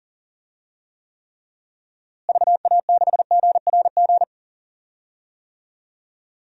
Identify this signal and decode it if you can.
Morse code, heard as a keyed tone, about 40 wpm, 700 Hz. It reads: VA6GRG
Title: DXCC-100 Call Signs - 6 Characters - Repeated 3x - 40wpm